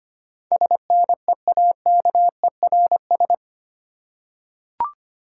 Morse code, heard as a keyed tone, about 25 wpm, 700 Hz. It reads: SNEAKERS
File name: practice / 25wpm / Easy Words with Hints - 25wpm